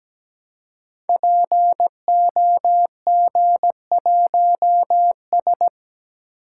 Morse code, heard as a keyed tone, about 17 wpm, 700 Hz. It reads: POG1S
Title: Code Groups - 5 Letter:Numbers - 17wpm